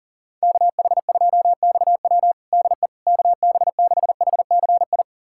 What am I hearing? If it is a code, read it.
Morse code, heard as a keyed tone, about 40 wpm, 700 Hz. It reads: KH2XW DE KB6HCI
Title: USA Callsign DE Callsign 40wpm